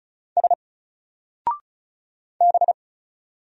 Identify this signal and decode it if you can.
Morse code, heard as a keyed tone, about 35 wpm, 700 Hz. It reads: S B
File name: Single-Letter-Number-35wpm